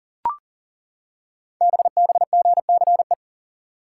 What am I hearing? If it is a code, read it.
Morse code, heard as a keyed tone, about 38 wpm, 700 Hz. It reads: BBGCE
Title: Code Groups - 5 Letter:Numbers - 40wpm